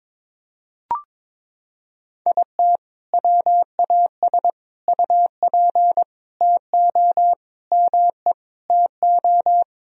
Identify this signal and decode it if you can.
Morse code, heard as a keyed tone, about 22 wpm, 700 Hz. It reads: IT WAS UP TO ME TO
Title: Sentences from Top 1000 Words - Review Part 2 - 22wpm